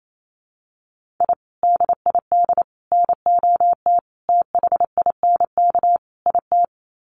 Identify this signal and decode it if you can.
Morse code, heard as a keyed tone, about 28 wpm, 700 Hz. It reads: I DID NOT THINK IT
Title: Sets of 5 Words Encore2 28wpm